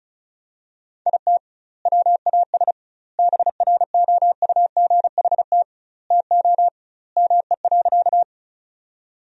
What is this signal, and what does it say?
Morse code, heard as a keyed tone, about 34 wpm, 700 Hz. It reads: IT WAS BROUGHT TO ME.
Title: 2x Word-Spacing - Sentences from Top 500 Words - 35wpm